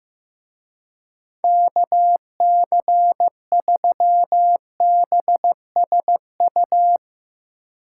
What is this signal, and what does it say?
Morse code, heard as a keyed tone, about 15 wpm, 700 Hz. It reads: KC3BSU
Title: Speed Racing - USA 2x3 Call Signs 15wpm